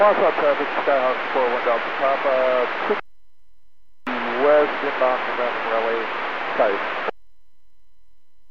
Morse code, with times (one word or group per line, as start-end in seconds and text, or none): none